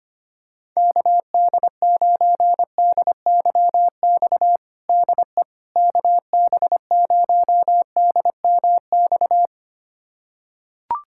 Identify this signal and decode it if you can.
Morse code, heard as a keyed tone, about 25 wpm, 700 Hz. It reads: KD9DYX DE KB0DMX
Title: USA Callsign DE Callsign 25wpm